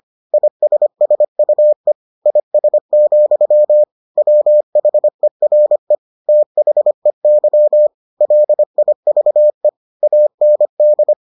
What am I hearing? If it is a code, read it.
Morse code, heard as a keyed tone, about 25 wpm, 600 Hz. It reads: ISSUE IS, WHERE THEY LIVE AND